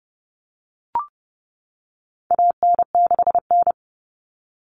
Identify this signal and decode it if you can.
Morse code, heard as a keyed tone, about 30 wpm, 700 Hz. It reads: AN6N
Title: Common Cut Numbers - 4-Digit 30wpm